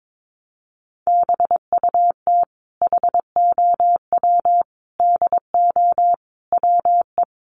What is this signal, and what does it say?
Morse code, heard as a keyed tone, about 22 wpm, 700 Hz. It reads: BUT HOW DO WE
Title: Sentences from Top 700 Words 22wpm